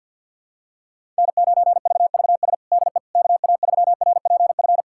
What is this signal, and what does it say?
Morse code, heard as a keyed tone, about 50 wpm, 700 Hz. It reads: N9VVS DE KA3RPF
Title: USA Callsign DE Callsign 50wpm